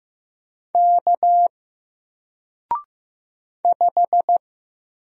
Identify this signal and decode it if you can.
Morse code, heard as a keyed tone, about 15 wpm, 700 Hz. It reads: K 5
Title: Single Letter-Number 15wpm